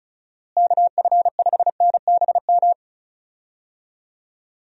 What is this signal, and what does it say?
Morse code, heard as a keyed tone, about 36 wpm, 700 Hz. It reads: KF5NBM
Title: USA Call Signs - 6 Characters - Repeated 2x - 35wpm